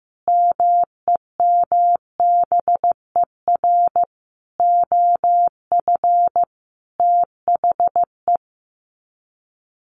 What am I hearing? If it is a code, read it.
Morse code, heard as a keyed tone, about 15 wpm, 700 Hz. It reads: MEMBER OF THE
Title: Sets of 5 Words 15wpm